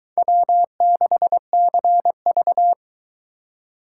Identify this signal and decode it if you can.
Morse code, heard as a keyed tone, about 23 wpm, 700 Hz. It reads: W6CV